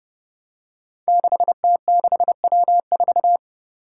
Morse code, heard as a keyed tone, about 31 wpm, 700 Hz. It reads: 6T6W4
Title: Five Challenging Letter-Number Combinations 30wpm